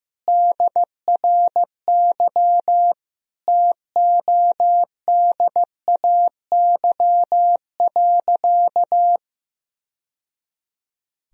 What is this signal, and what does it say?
Morse code, heard as a keyed tone, about 15 wpm, 700 Hz. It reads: DRY TODAY.
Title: Sentences from Top 1000 Words - Review Part 3 - 15wpm